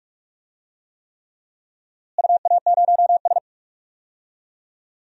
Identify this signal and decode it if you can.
Morse code, heard as a keyed tone, about 46 wpm, 700 Hz. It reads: UA0S
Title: DXCC-100 Call Signs - 4 Characters - Repeated 3x - 45wpm